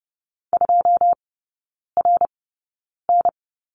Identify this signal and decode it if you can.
Morse code, heard as a keyed tone, about 30 wpm, spaced as for 9 wpm, 700 Hz. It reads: 2RN
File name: course / 30:10wpm / 087 Lesson 15e - Number 5 - Three Characters 30wpm